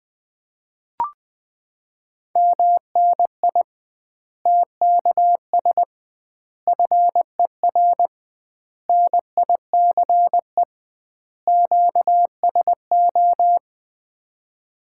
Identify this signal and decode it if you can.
Morse code, heard as a keyed tone, about 20 wpm, 700 Hz. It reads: MNI TKS FER NICE QSO
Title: Common QSO Phrases - 2x Word-Spacing - 20wpm